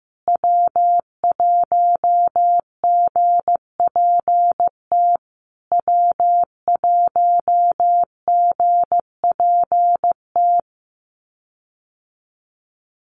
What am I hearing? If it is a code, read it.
Morse code, heard as a keyed tone, about 15 wpm, 700 Hz. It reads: W1GPT W1GPT
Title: USA Call Signs Repeated 3x 15wpm